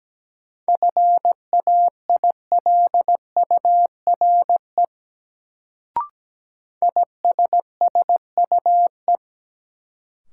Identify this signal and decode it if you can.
Morse code, heard as a keyed tone, about 17 wpm, 700 Hz. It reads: FAILURE ISSUE
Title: Sentences from Top 900 Words - New Word Drill - 17wpm